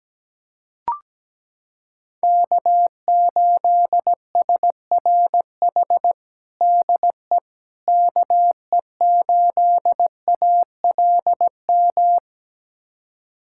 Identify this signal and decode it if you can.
Morse code, heard as a keyed tone, about 17 wpm, 700 Hz. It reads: K8SRH DE KE8ALM
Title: USA Callsign DE Callsign 17wpm